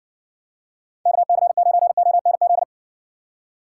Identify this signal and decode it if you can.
Morse code, heard as a keyed tone, about 60 wpm, 700 Hz. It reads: KC9ONZ